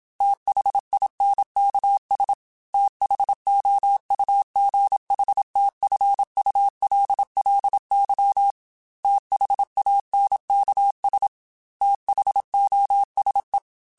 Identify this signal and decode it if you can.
Morse code, heard as a keyed tone, about 26 wpm, 800 Hz. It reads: THINKS THOUGHTFULLY THANKS THOSE
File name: TH Runs / TH Runs mp3s / TH Runs @25